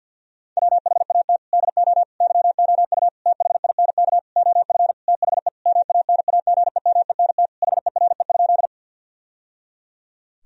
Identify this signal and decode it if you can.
Morse code, heard as a keyed tone, about 52 wpm, 700 Hz. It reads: WHAT DO YOU THINK OF THE MANAGEMENT HERE?
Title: Sentences from Top 1000 Words - Review Part 4 - 50wpm